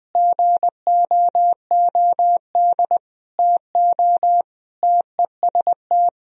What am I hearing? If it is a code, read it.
Morse code, heard as a keyed tone, about 20 wpm, 700 Hz. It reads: GOOD TO TEST
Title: Sentences from Top 1000 Words - Review Part 3 - 20wpm